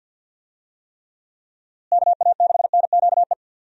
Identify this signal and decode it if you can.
Morse code, heard as a keyed tone, about 47 wpm, 700 Hz. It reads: KA6NQE
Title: USA Call Signs - 6 Characters - Repeated 3x - 50wpm